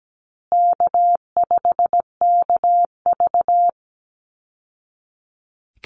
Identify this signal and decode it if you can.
Morse code, heard as a keyed tone, about 17 wpm, 700 Hz. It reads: K5KV
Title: Call Signs Repeated 3x - 17wpm